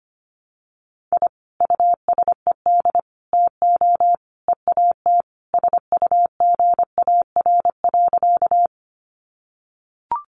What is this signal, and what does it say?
Morse code, heard as a keyed tone, about 25 wpm, 700 Hz. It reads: I USED TO EAT SUGAR.
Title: Sentences from Top 1200 Words 25wpm